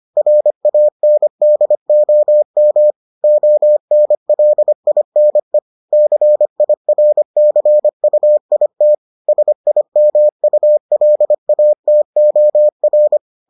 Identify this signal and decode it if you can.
Morse code, heard as a keyed tone, about 25 wpm, 600 Hz. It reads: RANDOM ONLINE CIRCUIT SIMULATOR